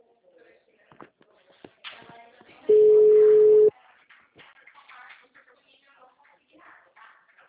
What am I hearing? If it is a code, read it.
Ringback tone (Europe, 425 Hz), with speech.
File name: out- -2039- - - .17